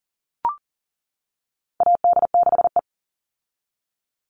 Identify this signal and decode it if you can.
Morse code, heard as a keyed tone, about 40 wpm, 700 Hz. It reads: AD6E
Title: Call Signs - 40wpm